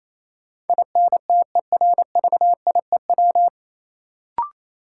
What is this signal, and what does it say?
Morse code, heard as a keyed tone, about 28 wpm, 700 Hz. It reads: INTERVIEW
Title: Sentences from Top 1600 Words - New Word Drill - 28wpm